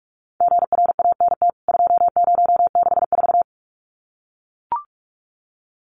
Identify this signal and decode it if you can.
Morse code, heard as a keyed tone, about 44 wpm, 700 Hz. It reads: GRANT 2064
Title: CWT Exchange 45wpm